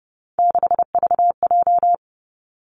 Morse code, heard as a keyed tone, about 30 wpm, 700 Hz. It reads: 6VJ
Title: Three Challenging Letter-Number Combinations 30wpm